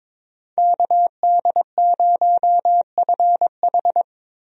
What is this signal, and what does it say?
Morse code, heard as a keyed tone, about 22 wpm, 700 Hz. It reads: KD0FH
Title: USA Call Signs - 5 Characters - Repeated 2x - 22wpm